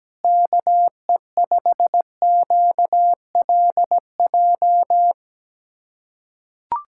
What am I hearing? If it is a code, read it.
Morse code, heard as a keyed tone, about 17 wpm, 700 Hz. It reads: KE5QLJ
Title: USA Callsign DE Callsign - 6 Characters - 17wpm